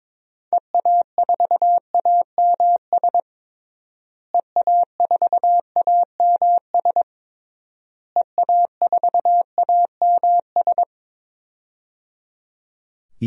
Morse code, heard as a keyed tone, about 21 wpm, 700 Hz. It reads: EA4AMS EA4AMS EA4AMS
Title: DXCC-100 Call Signs - 6 Characters - Repeated 3x - 22wpm